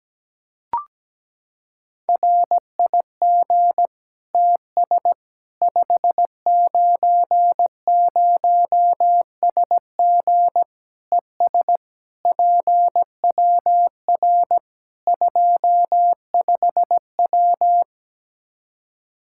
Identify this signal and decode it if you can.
Morse code, heard as a keyed tone, about 17 wpm, 700 Hz. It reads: RIG TS 590SG ES PWR 25W
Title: K1USN SST QSO Practice 17wpm